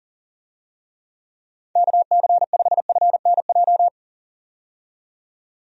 Morse code, heard as a keyed tone, about 40 wpm, 700 Hz. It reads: KC5FNJ
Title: USA Call Signs - 6 Characters - 40wpm